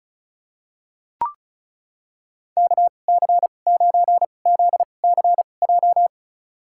Morse code, heard as a keyed tone, about 36 wpm, spaced as for 25 wpm, 700 Hz. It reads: KC9ZCJ
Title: USA Call Signs - 6 Characters - 2x Character Spacing - 35wpm